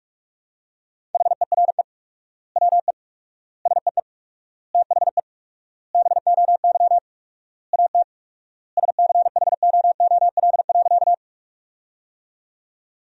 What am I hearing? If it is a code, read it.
Morse code, heard as a keyed tone, about 47 wpm, 700 Hz. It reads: HERE WE SEE THE BOY AT SCHOOL.
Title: 4x Word-Spacing - Sentences from Top 300 Words - 45wpm